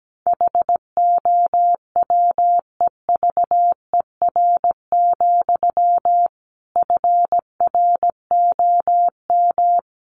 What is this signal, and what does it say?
Morse code, heard as a keyed tone, about 17 wpm, 700 Hz. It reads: HOWEVER, FROM
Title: Sentences from Top 1300 Words 17wpm